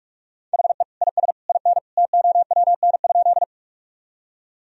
Morse code, heard as a keyed tone, about 44 wpm, 700 Hz. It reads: HE IS IN TOWN?
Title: Sentences from Top 1000 Words - Review Part 2 - 45wpm